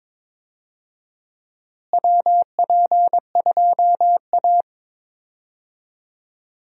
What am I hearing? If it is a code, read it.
Morse code, heard as a keyed tone, about 22 wpm, 700 Hz. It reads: WP2A
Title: USA Call Signs - 4 Characters - Repeated 3x - 22wpm